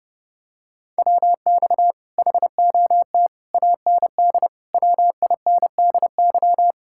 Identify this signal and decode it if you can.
Morse code, heard as a keyed tone, about 31 wpm, 700 Hz. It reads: WX HOT AND WINDY